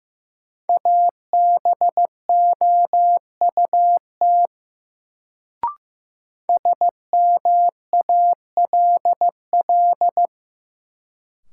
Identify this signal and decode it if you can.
Morse code, heard as a keyed tone, about 15 wpm, 700 Hz. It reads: ABOUT SMALL
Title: Sets of 2 Words - New Word Drill - 15wpm